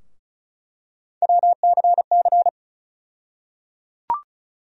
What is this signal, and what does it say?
Morse code, heard as a keyed tone, about 36 wpm, 700 Hz. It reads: WCC
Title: USA Call Sign Postfixes - 3 Letters 35wpm